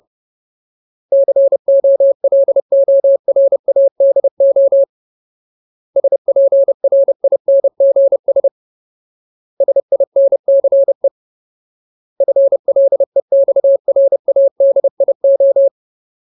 Morse code, heard as a keyed tone, about 30 wpm, 550 Hz. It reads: COLORADO SPRINGS SINCE FLEXRADIO
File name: Ultimate Word List / 30WPM Ultimate